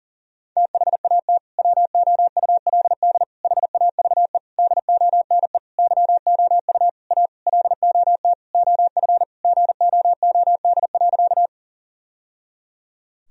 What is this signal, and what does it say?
Morse code, heard as a keyed tone, about 37 wpm, 700 Hz. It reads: THAT WOULD HAVE DONE YOU A LOT OF GOOD.
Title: Sentences from Top 1000 Words - Review Part 3 - 40wpm